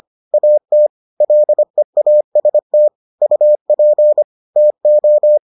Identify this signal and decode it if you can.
Morse code, heard as a keyed tone, about 25 wpm, 600 Hz. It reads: AT LEAST UP TO